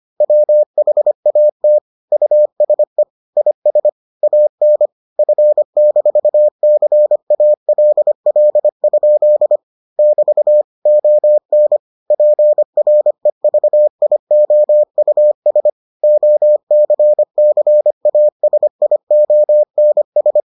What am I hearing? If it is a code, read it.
Morse code, heard as a keyed tone, about 25 wpm, 600 Hz. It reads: WHAT USE IS AN F-CALL? <BT> ON PREVIOUS OCCASIONS